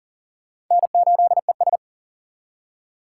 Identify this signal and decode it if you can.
Morse code, heard as a keyed tone, about 40 wpm, 700 Hz. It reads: N8ES
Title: USA Call Signs - 4 Characters - 2x Character Spacing - 40wpm